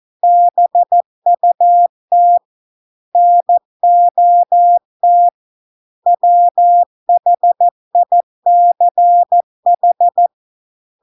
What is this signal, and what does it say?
Morse code, heard as a keyed tone, about 14 wpm, 700 Hz. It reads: BUT NOT WHICH